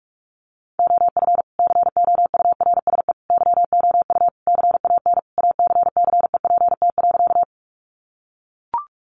Morse code, heard as a keyed tone, about 44 wpm, 700 Hz. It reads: OF COURSE YOU CAN ACCEPT.
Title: Sentences from Top 1300 Words 45wpm